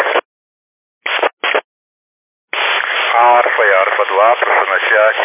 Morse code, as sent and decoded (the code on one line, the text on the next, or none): none